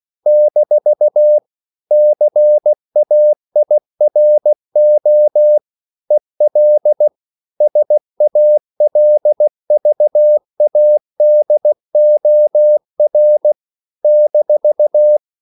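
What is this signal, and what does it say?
Morse code, heard as a keyed tone, about 16 wpm, 600 Hz. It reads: - CAIRO EL SALVADOR -
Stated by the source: CW practice of national capitals at 16 words per minute, from ON6ZQ.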